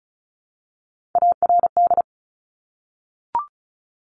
Morse code, heard as a keyed tone, about 36 wpm, 700 Hz. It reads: ARD
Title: Top Three Letter Combinations in Words 35wpm